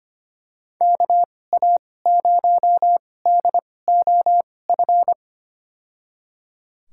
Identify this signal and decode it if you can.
Morse code, heard as a keyed tone, about 25 wpm, spaced as for 18 wpm, 700 Hz. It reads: KA0DOF
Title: USA Call Signs - 6 Characters - 2x Character Spacing - 25wpm